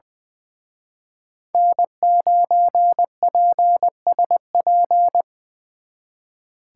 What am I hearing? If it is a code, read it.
Morse code, heard as a keyed tone, about 24 wpm, 700 Hz. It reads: N9PSP
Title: Speed Racing - USA 1x3 Call Signs 20wpm